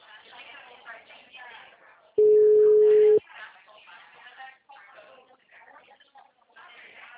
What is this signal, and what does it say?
Ringback tone (Europe, 425 Hz), with speech.